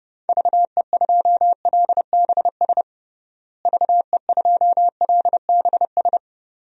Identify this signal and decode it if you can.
Morse code, heard as a keyed tone, about 30 wpm, 700 Hz. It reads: VE2LBS VE2LBS
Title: DXCC-100 Call Signs - 6 Characters - Repeated 2x - 30wpm